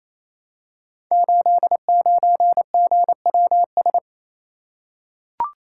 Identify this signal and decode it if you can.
Morse code, heard as a keyed tone, about 29 wpm, 700 Hz. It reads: 89GWS
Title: Code Groups - 5 Letter:Numbers - 28wpm